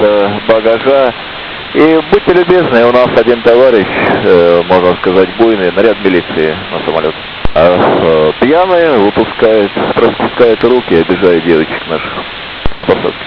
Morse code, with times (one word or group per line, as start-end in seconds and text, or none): none